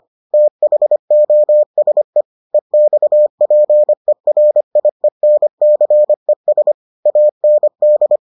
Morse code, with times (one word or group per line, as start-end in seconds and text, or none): 0.3-2.2 THOSE
2.5-6.7 EXPERIENCES
7.1-8.2 AND